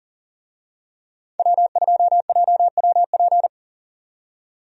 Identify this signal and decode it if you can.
Morse code, heard as a keyed tone, about 39 wpm, 700 Hz. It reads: W2JWP
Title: USA Call Signs - 5 Characters - 40wpm